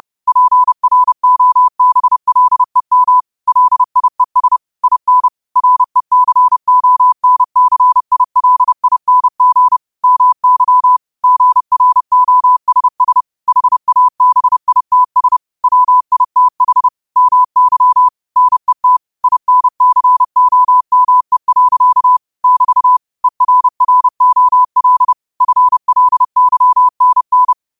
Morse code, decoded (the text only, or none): PROBLEM LIES IN RECONCILING MY GROSS HABITS WITH MY NET INCOME. <BT> ERROL FLYNN